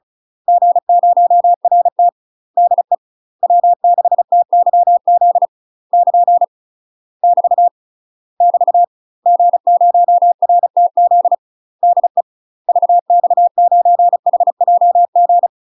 Morse code, heard as a keyed tone, about 34 wpm, 700 Hz. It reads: G0RT DE W6TYZ <KN> <BT> <BT> G0RTZ DE VX9HJG
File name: UK MASTER QSOs TRAINING / 35WPM / UK MASTER QSOs 0003